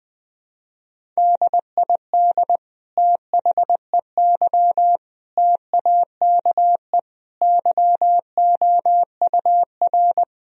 Morse code, heard as a keyed tone, about 20 wpm, 700 Hz. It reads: DID THEY TAKE YOUR
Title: Sentences from Top 1000 Words - Review Part 3 - 20wpm